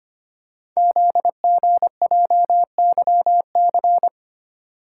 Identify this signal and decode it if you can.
Morse code, heard as a keyed tone, about 25 wpm, 700 Hz. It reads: ZGJYC
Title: Code Groups - 5 Letter:Numbers - 25wpm